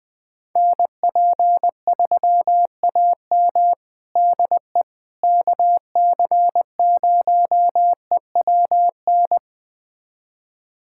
Morse code, heard as a keyed tone, about 20 wpm, 700 Hz. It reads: NP3AM DE KC0EWN
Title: USA Callsign DE Callsign 20wpm